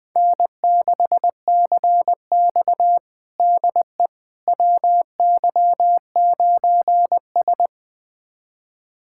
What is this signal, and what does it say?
Morse code, heard as a keyed tone, about 20 wpm, 700 Hz. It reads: N6CX DE WY9S
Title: USA Callsign DE Callsign - 4 Characters - 20wpm